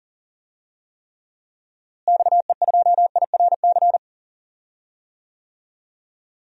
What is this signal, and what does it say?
Morse code, heard as a keyed tone, about 40 wpm, 700 Hz. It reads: XE2IRC
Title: DXCC-100 Call Signs - 6 Characters - Repeated 3x - 40wpm